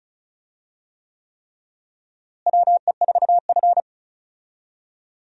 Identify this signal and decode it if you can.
Morse code, heard as a keyed tone, about 36 wpm, 700 Hz. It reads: WE4F
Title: USA Call Signs - 4 Characters - Repeated 3x - 35wpm